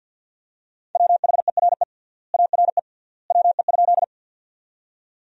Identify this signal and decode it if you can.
Morse code, heard as a keyed tone, about 50 wpm, 700 Hz. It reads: WHERE ARE WE?
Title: 3x Word-Spacing - Sentences from Top 200 Words - 50wpm